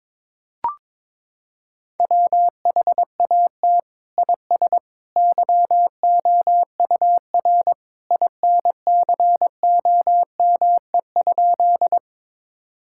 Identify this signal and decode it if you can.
Morse code, heard as a keyed tone, about 22 wpm, 700 Hz. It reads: WHAT IS YOUR INCOME?
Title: Sentences from Top 1000 Words - Review Part 5 - 22wpm